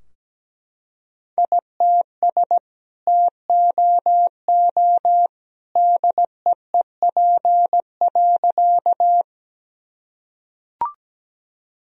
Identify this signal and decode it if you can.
Morse code, heard as a keyed tone, about 17 wpm, 700 Hz. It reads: ITS TOO DEEP.
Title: Sentences from Top 1000 Words - Review Part 3 - 17wpm